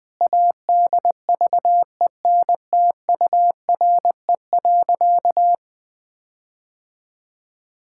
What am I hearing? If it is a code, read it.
Morse code, heard as a keyed tone, about 19 wpm, 700 Hz. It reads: ADVENTURE.
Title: Sentences with 4-Word Phrases - Spoken Beginning - 20wpm